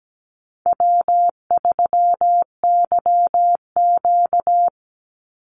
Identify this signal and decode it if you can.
Morse code, heard as a keyed tone, about 17 wpm, 700 Hz. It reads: W3YQ
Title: Easy Call Signs Repeated 2x - 17wpm